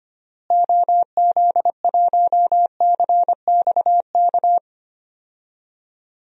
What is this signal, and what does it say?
Morse code, heard as a keyed tone, about 25 wpm, 700 Hz. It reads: OZ1CXK
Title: DXCC-100 Call Signs - 6 Characters - Repeated 3x - 25wpm